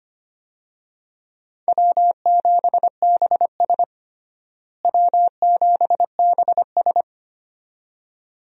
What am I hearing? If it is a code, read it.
Morse code, heard as a keyed tone, about 25 wpm, 700 Hz. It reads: W7BS W7BS
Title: USA Call Signs - 4 Characters - Repeated 2x - 25wpm